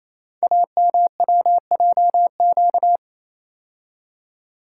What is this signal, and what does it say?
Morse code, heard as a keyed tone, about 29 wpm, 700 Hz. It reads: AMWJQ
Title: Code Groups - 5 Letters - 28wpm